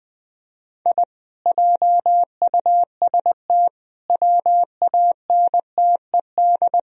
Morse code, heard as a keyed tone, about 20 wpm, 700 Hz. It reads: I JUST WANTED
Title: Speed-Racing - Sets of 4 Words 17wpm